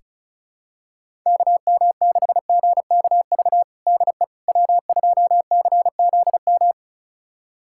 Morse code, heard as a keyed tone, about 36 wpm, 700 Hz. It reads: KM6GKV DE W2CZM
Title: USA Callsign DE Callsign 35wpm